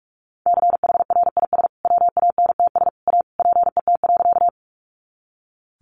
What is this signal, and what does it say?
Morse code, heard as a keyed tone, about 44 wpm, 700 Hz. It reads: CHRIS WANTS A PET.
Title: Additional Easy Phrases 45wpm